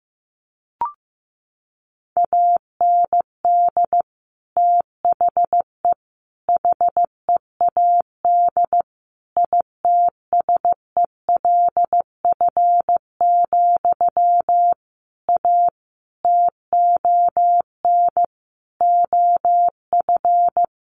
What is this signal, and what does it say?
Morse code, heard as a keyed tone, about 15 wpm, 700 Hz. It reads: AND THE HEAD ITSELF, A TON OF
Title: A Sound Of Thunder 15wpm